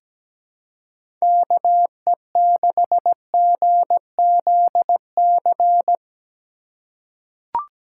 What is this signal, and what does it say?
Morse code, heard as a keyed tone, about 17 wpm, 700 Hz. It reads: KE6GZC
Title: USA Call Signs - 6 Characters - 3x Character Spacing - 17wpm